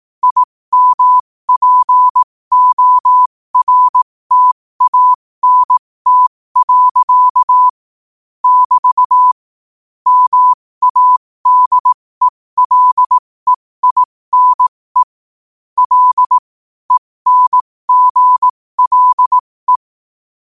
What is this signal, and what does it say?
Morse code, heard as a keyed tone, about 18 wpm, spaced as for 15 wpm, 1000 Hz. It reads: IMPORTANT. <BT> MADELEINE L ENGLE
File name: qotdfiles / 15wpm